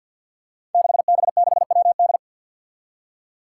Morse code, heard as a keyed tone, about 47 wpm, 700 Hz. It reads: 6B6WD
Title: Five Challenging Letter-Number Combinations 50wpm